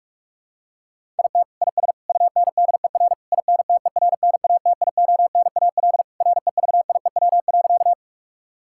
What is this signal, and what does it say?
Morse code, heard as a keyed tone, about 46 wpm, 700 Hz. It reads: IT IS UNDER INTERNATIONAL REVIEW.
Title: Sentences from Top 1000 Words - Review Part 6 - 45wpm